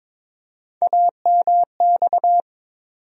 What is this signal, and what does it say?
Morse code, heard as a keyed tone, about 22 wpm, 700 Hz. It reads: AMX
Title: USA Call Sign Postfixes - 3 Letters 22wpm